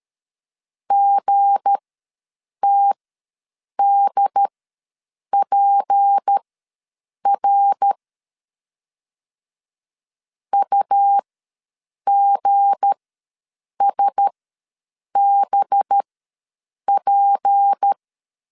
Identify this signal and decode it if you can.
Morse code, heard as a keyed tone, about 13 wpm, spaced as for 7 wpm, 800 Hz. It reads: GTDPR UGSBP
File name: LEZIONI / 7 GF9Z / 7 G ws 12wpm 3140